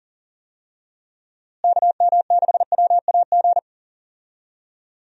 Morse code, heard as a keyed tone, about 37 wpm, 700 Hz. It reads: KM6WAG
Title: USA Call Signs - 6 Characters - Repeated 3x - 40wpm